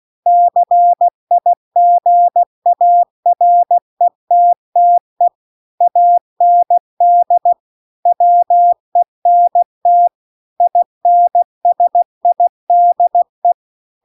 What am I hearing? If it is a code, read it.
Morse code, heard as a keyed tone, about 16 wpm, 700 Hz. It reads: CIGARETTE AND WENT INSIDE